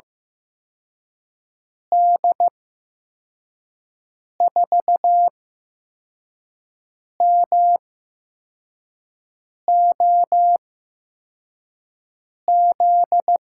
Morse code, heard as a keyed tone, about 15 wpm, spaced as for 4 wpm, 700 Hz. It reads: D4MOZ
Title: USA Call Signs - 8x Character Spacing - 15wpm